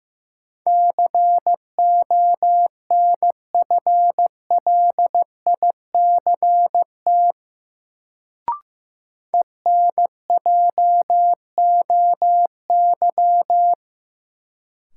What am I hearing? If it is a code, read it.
Morse code, heard as a keyed tone, about 15 wpm, 700 Hz. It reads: CONFLICT ENJOY
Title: Sentences from Top 1300 Words - New Word Drill Alt - 15wpm